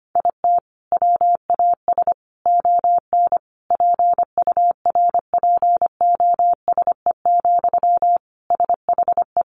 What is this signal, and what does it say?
Morse code, heard as a keyed tone, about 25 wpm, 700 Hz. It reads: IT WAS ON PURPOSE, SHE
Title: Sentences from Top 800 Words 25wpm